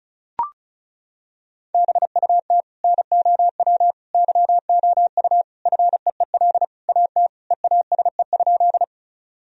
Morse code, heard as a keyed tone, about 34 wpm, 700 Hz. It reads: BUT NOW YOU FEEL AT EASE?
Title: Sentences from Top 1000 Words - Review Part 1 - 35wpm